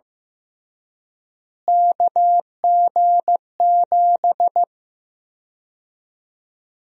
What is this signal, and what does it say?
Morse code, heard as a keyed tone, about 15 wpm, 700 Hz. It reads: KG7
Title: USA Call Sign Prefixes - Repeated 2x - 15wpm